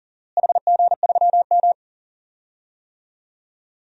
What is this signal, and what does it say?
Morse code, heard as a keyed tone, about 40 wpm, 690 Hz. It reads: HG3M
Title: DXCC-100 Call Signs - 4 Characters - Repeated 2x - 40wpm